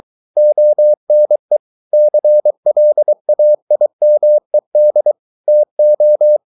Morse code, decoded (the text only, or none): ONE CLAIMED TO